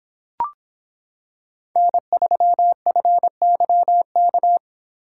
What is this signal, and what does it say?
Morse code, heard as a keyed tone, about 26 wpm, 700 Hz. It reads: N3FYK